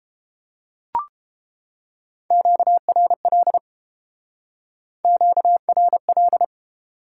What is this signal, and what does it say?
Morse code, heard as a keyed tone, about 30 wpm, 700 Hz. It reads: QRL QRL